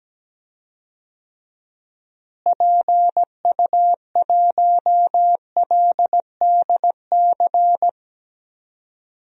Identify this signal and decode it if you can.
Morse code, heard as a keyed tone, about 17 wpm, 700 Hz. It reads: PU1LDC
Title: DXCC-100 Call Signs - 6 Characters - Repeated 2x - 17wpm